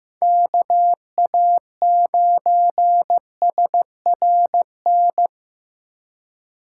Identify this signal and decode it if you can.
Morse code, heard as a keyed tone, about 15 wpm, 700 Hz. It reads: KA9SRN
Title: USA Call Signs - 6 Characters - 4x Character Spacing - 15wpm